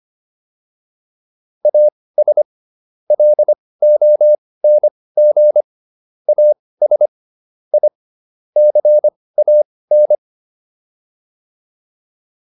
Morse code, heard as a keyed tone, about 25 wpm, spaced as for 18 wpm, 600 Hz. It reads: AS LONG AS I CAN